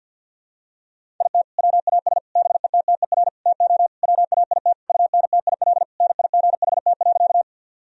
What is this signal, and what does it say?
Morse code, heard as a keyed tone, about 52 wpm, 700 Hz. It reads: IT WAS BETTER TO WAIT UNTIL NIGHT.
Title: Sentences from Top 1000 Words - Review Part 1 - 50wpm